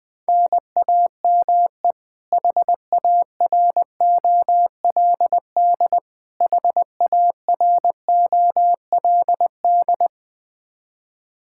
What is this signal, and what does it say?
Morse code, heard as a keyed tone, about 20 wpm, 700 Hz. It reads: NAME HAROLD HAROLD
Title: Practice QSO - Exchange 1 - Beginning - 20wpm